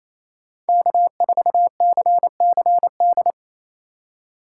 Morse code, heard as a keyed tone, about 28 wpm, 700 Hz. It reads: K4CCD
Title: USA Call Signs - 4x Character Spacing - 28wpm